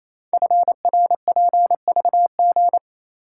Morse code, heard as a keyed tone, about 27 wpm, 700 Hz. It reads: FRPVG
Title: Code Groups - 5 Letters - 28wpm